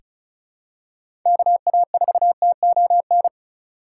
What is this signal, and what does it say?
Morse code, heard as a keyed tone, about 36 wpm, 700 Hz. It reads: KA4TON
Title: USA Call Signs - 3x Character Spacing - 35wpm